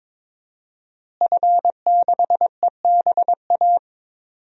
Morse code, heard as a keyed tone, about 22 wpm, 700 Hz. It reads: F6EBA
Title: DXCC-100 Call Signs - 5 Characters - Repeated 3x - 22wpm